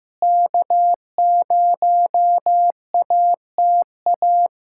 Morse code, heard as a keyed tone, about 15 wpm, 700 Hz. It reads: K0ATA
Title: Practice QSO - Exchange 4 - Ending - 15wpm